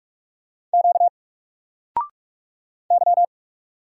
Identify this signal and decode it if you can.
Morse code, heard as a keyed tone, about 46 wpm, 700 Hz. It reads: Q Y
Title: Single Letter-Number 45wpm